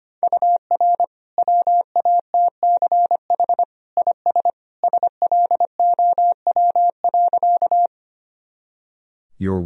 Morse code, heard as a keyed tone, about 25 wpm, 700 Hz. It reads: UR WATCH IS SLOW.
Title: Sentences from Top 400 Words and Abbreviations 25wpm